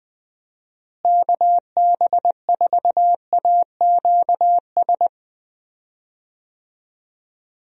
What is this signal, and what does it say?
Morse code, heard as a keyed tone, about 20 wpm, 700 Hz. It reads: KB4AQS
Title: USA Call Signs - 6 Characters - Repeated 2x - 20wpm